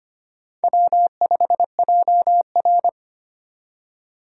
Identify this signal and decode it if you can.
Morse code, heard as a keyed tone, about 25 wpm, 700 Hz. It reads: W5JR
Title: USA Call Signs - 4 Characters - 25wpm